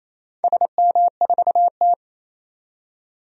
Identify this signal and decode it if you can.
Morse code, heard as a keyed tone, about 28 wpm, 700 Hz. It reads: SM4T
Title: DXCC-100 Call Signs - 4 Characters - 28wpm